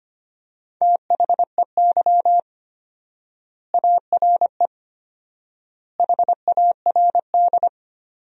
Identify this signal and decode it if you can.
Morse code, heard as a keyed tone, about 25 wpm, 700 Hz. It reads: THEY ARE HARD
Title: 4x Word-Spacing - Sentences from Top 300 Words - 25wpm